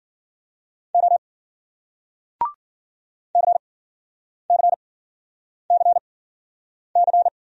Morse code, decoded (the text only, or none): O CCCC